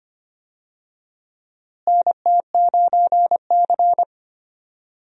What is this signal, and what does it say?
Morse code, heard as a keyed tone, about 25 wpm, 700 Hz. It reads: NT9C
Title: USA Call Signs - 4 Characters - Repeated 2x - 25wpm